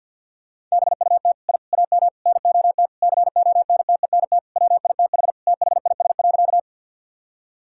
Morse code, heard as a keyed tone, about 50 wpm, 700 Hz. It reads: BUT I AM NOT CONTENT WITH THIS.
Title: Sentences from Top 1000 Words - Review Part 6 - 50wpm